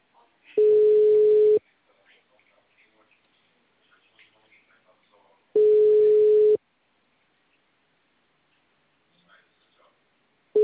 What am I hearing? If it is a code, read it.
Ringback tone (Europe, 425 Hz), with speech.